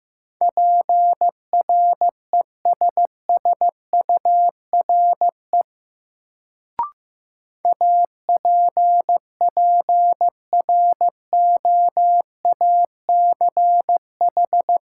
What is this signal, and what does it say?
Morse code, heard as a keyed tone, about 15 wpm, 700 Hz. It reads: PRESSURE APPROACH
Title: Sentences from Top 1000 Words - Review Part 6 - New Word Drill - 15wpm